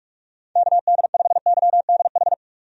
Morse code, heard as a keyed tone, about 44 wpm, 700 Hz. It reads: KD5YDH
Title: USA Call Signs - 6 Characters - 45wpm